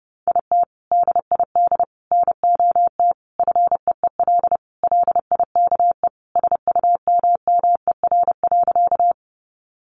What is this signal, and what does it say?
Morse code, heard as a keyed tone, about 30 wpm, 700 Hz. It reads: IT DID NOT FEEL LIKE SUMMER.
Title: Sentences from Top 1100 Words 30wpm